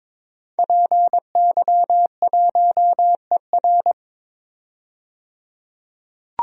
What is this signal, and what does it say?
Morse code, heard as a keyed tone, about 22 wpm, 700 Hz. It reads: PY1ER
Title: DXCC-100 Call Signs - 5 Characters - 22wpm